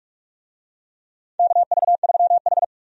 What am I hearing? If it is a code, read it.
Morse code, heard as a keyed tone, about 48 wpm, 700 Hz. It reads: KV3H
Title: USA Call Signs - 4 Characters - 45wpm